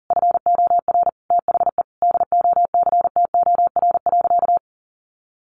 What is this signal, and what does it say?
Morse code, heard as a keyed tone, about 40 wpm, 700 Hz. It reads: FOR THE DOCTOR.
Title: Sentences from Top 1500 Words 40wpm